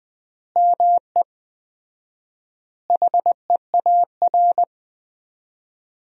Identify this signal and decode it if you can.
Morse code, heard as a keyed tone, about 20 wpm, 700 Hz. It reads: ME HEAR
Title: 4x Word-Spacing - Sentences from Top 300 Words - 20wpm